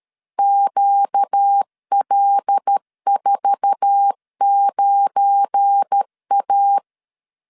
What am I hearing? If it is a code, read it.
Morse code, heard as a keyed tone, about 12 wpm, 800 Hz. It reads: QL49A
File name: LEZIONI / 10 SIGNS / 10C DOT 12wpm 2451